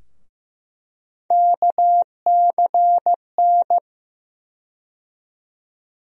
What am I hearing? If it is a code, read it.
Morse code, heard as a keyed tone, about 15 wpm, 700 Hz. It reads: KCN
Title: USA Call Sign Postfixes - 3 Letters 15wpm